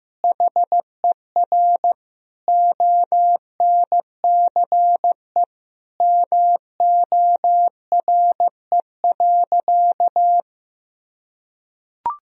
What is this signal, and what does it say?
Morse code, heard as a keyed tone, about 15 wpm, 700 Hz. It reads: HER ONCE MORE.
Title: Sentences from Top 500 Words and Abbreviations 15wpm